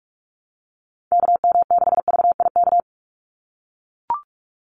Morse code, heard as a keyed tone, about 44 wpm, 700 Hz. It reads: KM6VIK
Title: USA Call Signs 45wpm